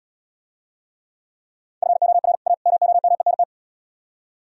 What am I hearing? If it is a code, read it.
Morse code, heard as a keyed tone, about 77 wpm, 700 Hz. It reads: FOR A MOMENT